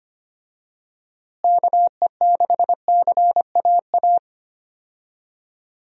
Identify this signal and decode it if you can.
Morse code, heard as a keyed tone, about 25 wpm, 700 Hz. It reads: KE6CAA
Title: USA Call Signs - 6 Characters - Repeated 3x - 25wpm